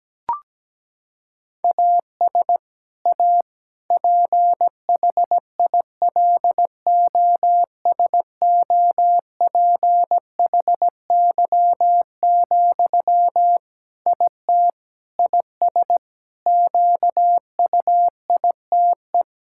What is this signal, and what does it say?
Morse code, heard as a keyed tone, about 17 wpm, 700 Hz. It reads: AS A PHILOSOPHY, IT IS QUITE
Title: Sentences from Top 1000 Words - Review Part 6 - 17wpm